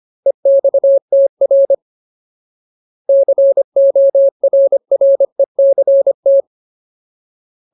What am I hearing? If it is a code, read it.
Morse code, heard as a keyed tone, about 25 wpm, 550 Hz. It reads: EXTR CORRECT
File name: Ultimate Word List / 25WPM Ultimate